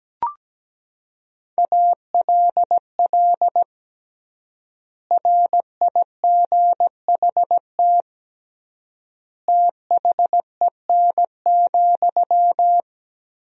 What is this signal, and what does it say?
Morse code, heard as a keyed tone, about 17 wpm, 700 Hz. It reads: ALL RIGHT THEN,
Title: 3x Word-Spacing - Sentences from Top 500 Words - 17wpm